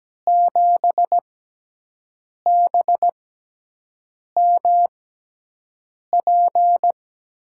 Morse code, heard as a keyed tone, about 17 wpm, spaced as for 6 wpm, 700 Hz. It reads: 7BMP
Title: USA Call Signs - 6 Characters - 6x Character Spacing - 17wpm